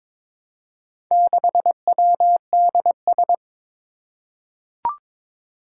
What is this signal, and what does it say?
Morse code, heard as a keyed tone, about 22 wpm, 700 Hz. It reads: 6WDS
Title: Four Challenging Letter-Number Combinations 22wpm